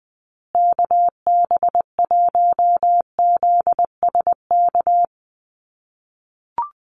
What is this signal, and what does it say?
Morse code, heard as a keyed tone, about 20 wpm, 700 Hz. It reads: KB1ZSK
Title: USA Call Signs Repeated 2x 20wpm